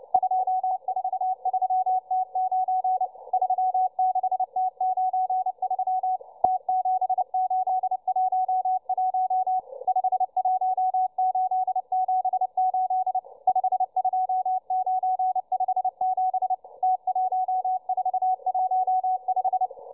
Morse code, heard as a keyed tone, about 29 wpm, 750 Hz, with radio noise. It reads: 243T9 36T93 T7811 51878 52957 T1415